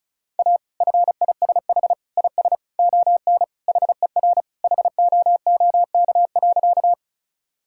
Morse code, heard as a keyed tone, about 34 wpm, 700 Hz. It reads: A FISH IS ON HER HOOK.
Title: Sentences from Top 1600 Words 35wpm